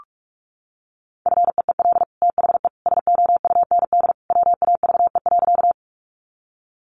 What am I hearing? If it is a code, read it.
Morse code, heard as a keyed tone, about 44 wpm, 700 Hz. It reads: FEEL THE SOUND WAVE.
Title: Sentences from Top 1400 Words 45wpm